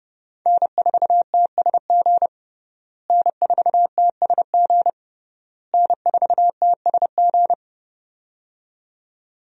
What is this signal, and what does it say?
Morse code, heard as a keyed tone, about 30 wpm, 700 Hz. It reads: N4TSG N4TSG N4TSG
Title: USA Call Signs - 5 Characters - Repeated 3x - 30wpm